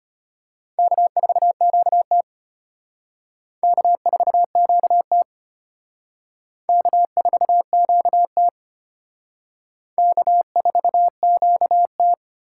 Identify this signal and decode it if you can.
Morse code, heard as a keyed tone, about 31 wpm, 700 Hz. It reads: K4QT K4QT K4QT K4QT
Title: Speed Racing - USA 1x2 Call Signs 25wpm